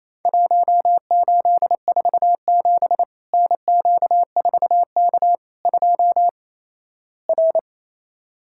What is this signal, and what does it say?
Morse code, heard as a keyed tone, about 28 wpm, 700 Hz. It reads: 1847 NQ4K 2 R
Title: Worked All Europe DX Contest - QTC Exchange - 28wpm